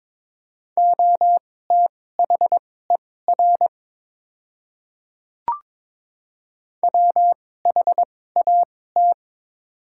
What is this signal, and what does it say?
Morse code, heard as a keyed tone, about 22 wpm, spaced as for 16 wpm, 700 Hz. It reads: OTHER WHAT
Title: 2x Character Spacing - Top 100 Words - 22wpm